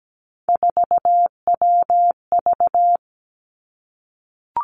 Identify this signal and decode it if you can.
Morse code, heard as a keyed tone, about 17 wpm, 700 Hz. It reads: 4WV
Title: Three Challenging Letter-Number Combinations 17wpm